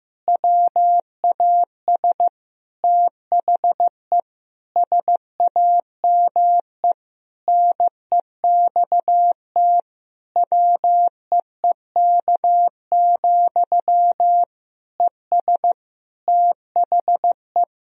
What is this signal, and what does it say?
Morse code, heard as a keyed tone, about 16 wpm, 700 Hz. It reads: WAS THE SAME NEXT WEEK, ES THE
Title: Sentences from Top 500 Words and Abbreviations 15wpm